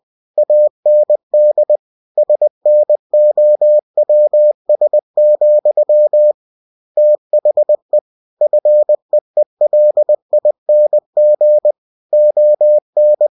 Morse code, heard as a keyed tone, about 20 wpm, 600 Hz. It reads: AND SNOWS, THE FEELING ON
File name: cw-books / winnie-the-pooh / winnie-20wpm-page-0068